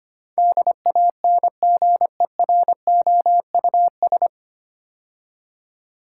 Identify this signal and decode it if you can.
Morse code, heard as a keyed tone, about 25 wpm, 700 Hz. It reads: DANGEROUS
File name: practice / 25wpm / 9 Letter Words - First 3 Letters Given - 25wpm